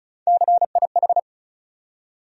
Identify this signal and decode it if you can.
Morse code, heard as a keyed tone, about 36 wpm, 700 Hz. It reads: CIH
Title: USA Call Sign Postfixes - 3 Letters 35wpm